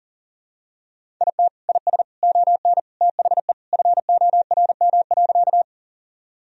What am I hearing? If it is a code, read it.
Morse code, heard as a keyed tone, about 37 wpm, 700 Hz. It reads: IT IS ON THE FORM.
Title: Sentences from Top 1000 Words - Review Part 1 - 40wpm